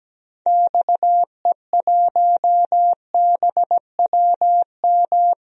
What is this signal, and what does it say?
Morse code, heard as a keyed tone, about 17 wpm, 700 Hz. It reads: XE1BWM
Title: DXCC-100 Call Signs - 6 Characters - 17wpm